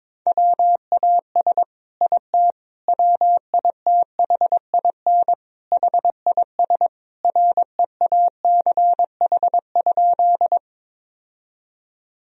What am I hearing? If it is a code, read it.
Morse code, heard as a keyed tone, about 22 wpm, 700 Hz. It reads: WAS IT WITHIN HIS REACH?
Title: Sentences from Top 1000 Words - Review Part 4 - 22wpm